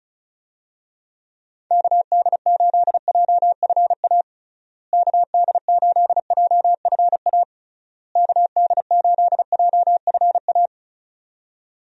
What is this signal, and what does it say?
Morse code, heard as a keyed tone, about 34 wpm, 700 Hz. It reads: KD8JFA KD8JFA KD8JFA
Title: USA Call Signs - 6 Characters - Repeated 3x - 35wpm